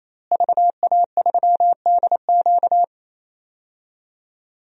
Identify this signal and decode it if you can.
Morse code, heard as a keyed tone, about 27 wpm, 700 Hz. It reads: VA3DQ
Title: DXCC-100 Call Signs - 5 Characters - Repeated 3x - 28wpm